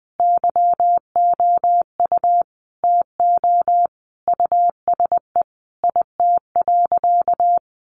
Morse code, heard as a keyed tone, about 20 wpm, 700 Hz. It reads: YOU TO USE IT.
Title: Sentences from Top 1100 Words 20wpm